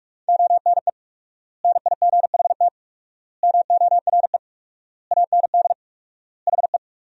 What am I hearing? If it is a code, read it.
Morse code, heard as a keyed tone, about 41 wpm, 700 Hz. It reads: ONE NIGHT MORE AND HE
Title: 4x Word-Spacing - Sentences from Top 300 Words - 45wpm